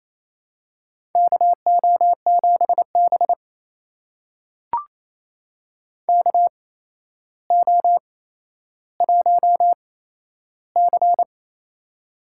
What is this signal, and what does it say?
Morse code, heard as a keyed tone, about 28 wpm, spaced as for 12 wpm, 700 Hz. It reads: KO7B KO1C
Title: USA Call Signs - 4 Characters - 8x Character Spacing - 28wpm